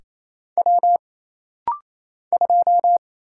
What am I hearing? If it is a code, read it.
Morse code, heard as a keyed tone, about 28 wpm, 700 Hz. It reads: W 2
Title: Single Letter-Number - Rapid-Fire 28wpm